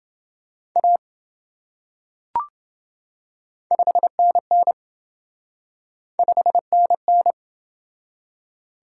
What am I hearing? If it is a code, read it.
Morse code, heard as a keyed tone, about 27 wpm, 700 Hz. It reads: A 5NN 5NN